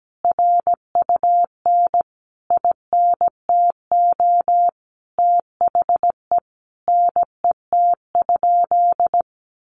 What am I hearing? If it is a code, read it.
Morse code, heard as a keyed tone, about 17 wpm, 700 Hz. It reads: RUN INTO THE NET?
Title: Sentences from Top 1400 Words 17wpm